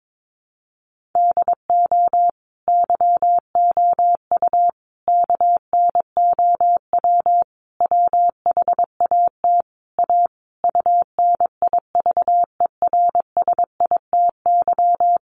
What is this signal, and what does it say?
Morse code, heard as a keyed tone, about 22 wpm, 700 Hz. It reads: DO YOU KNOW WHAT A UNIVERSITY
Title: Sentences from Top 1500 Words 22wpm